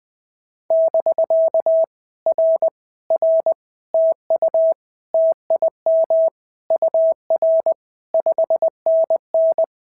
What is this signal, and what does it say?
Morse code, heard as a keyed tone, about 20 wpm, 650 Hz. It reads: <BK> R R TU TIM UR 5NN
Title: POTA QSO Practice - 20wpm